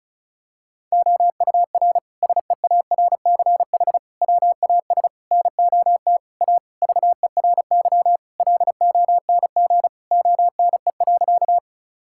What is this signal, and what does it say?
Morse code, heard as a keyed tone, about 34 wpm, 700 Hz. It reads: OUR SEARCH WAS NOT A VERY LONG ONE.
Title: Sentences from Top 1000 Words - Review Part 5 - 35wpm